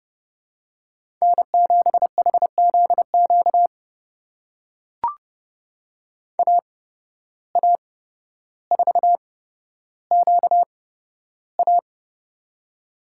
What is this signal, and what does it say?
Morse code, heard as a keyed tone, about 29 wpm, spaced as for 13 wpm, 700 Hz. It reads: N7HZQ AA4QA
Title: USA Call Signs - 5 Characters - 8x Character Spacing - 30wpm